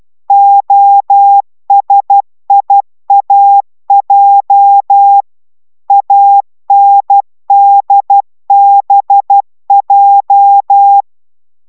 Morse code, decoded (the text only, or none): OSIAJ ANDBJ